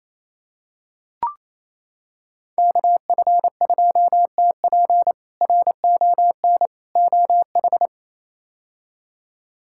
Morse code, heard as a keyed tone, about 28 wpm, 700 Hz. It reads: KF2TP RON OH
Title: K1USN SST QSO Practice - 28wpm